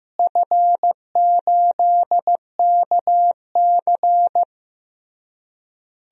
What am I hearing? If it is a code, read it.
Morse code, heard as a keyed tone, about 15 wpm, 700 Hz. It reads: F8KC
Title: DXCC-100 Call Signs - 4 Characters - Repeated 3x - 15wpm